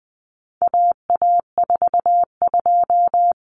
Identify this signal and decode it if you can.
Morse code, heard as a keyed tone, about 20 wpm, 700 Hz. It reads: AA42
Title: Common Cut Numbers - 4-Digit 20wpm